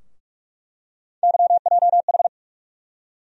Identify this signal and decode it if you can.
Morse code, heard as a keyed tone, about 46 wpm, 700 Hz. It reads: YJH
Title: USA Call Sign Postfixes - 3 Letters 45wpm